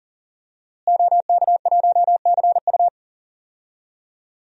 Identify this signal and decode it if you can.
Morse code, heard as a keyed tone, about 40 wpm, 700 Hz. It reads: OK1CU
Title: DXCC-100 Call Signs - 5 Characters - Repeated 2x - 40wpm